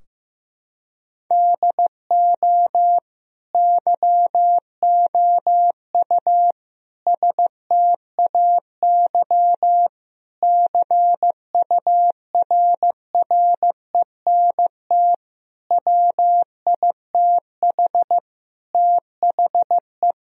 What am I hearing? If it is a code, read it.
Morse code, heard as a keyed tone, about 15 wpm, 700 Hz. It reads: DO YOU STAY CURRENT WITH THE
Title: Sentences from Top 1000 Words - Review Part 4 - 15wpm